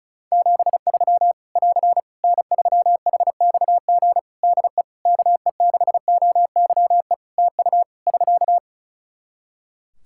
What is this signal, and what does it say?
Morse code, heard as a keyed tone, about 34 wpm, 700 Hz. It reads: 73 <AR> N3HXG DE KE6OYE TU <SK>
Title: Practice QSO - Exchange 4 - Ending - 35wpm